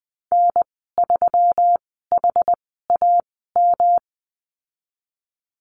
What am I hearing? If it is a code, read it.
Morse code, heard as a keyed tone, about 20 wpm, spaced as for 15 wpm, 700 Hz. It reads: N3HAM
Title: 122 Lesson 20g - Number 3 - Call Signs 20wpm